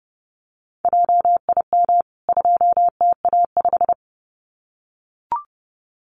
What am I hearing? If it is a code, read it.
Morse code, heard as a keyed tone, about 30 wpm, 700 Hz. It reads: JIM 2TA5
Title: CWT Exchange 30wpm